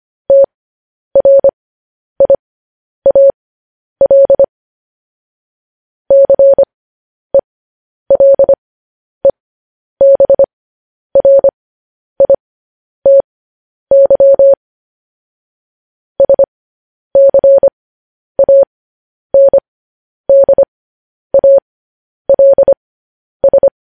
Morse code, heard as a keyed tone, about 25 wpm, spaced as for 10 wpm, 550 Hz. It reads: TRIAL CELEBRITY SCANDALS
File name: podcast / News headlines-1x-25x10